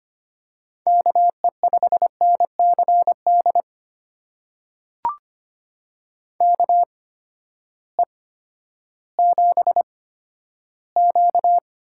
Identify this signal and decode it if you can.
Morse code, heard as a keyed tone, about 25 wpm, spaced as for 13 wpm, 700 Hz. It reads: KE5NCD KE7Q
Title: USA Call Signs - 6 Characters - 8x Character Spacing - 25wpm